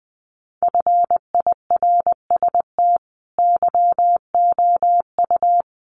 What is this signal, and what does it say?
Morse code, heard as a keyed tone, about 20 wpm, 700 Hz. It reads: FIRST YOU
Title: Sentences from Top 1500 Words 20wpm